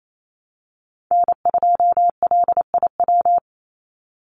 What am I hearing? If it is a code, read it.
Morse code, heard as a keyed tone, about 28 wpm, 700 Hz. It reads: N2LIW
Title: USA Call Signs 28wpm